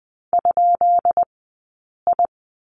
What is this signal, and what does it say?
Morse code, heard as a keyed tone, about 20 wpm, 700 Hz. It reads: ? I
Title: 132 Lesson 22d - Question Mark - Two Characters 20wpm